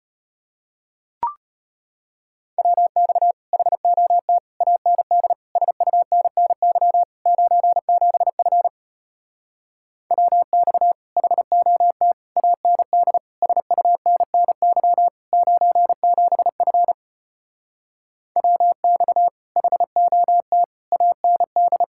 Morse code, heard as a keyed tone, about 33 wpm, 700 Hz. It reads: WX HOT AND SUNNY 97F WX HOT AND SUNNY 97F WX HOT AND